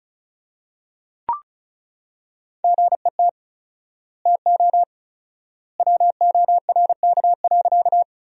Morse code, heard as a keyed tone, about 36 wpm, 700 Hz. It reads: GET TO WORK.
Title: 4x Word-Spacing - Sentences from Top 200 Words - 35wpm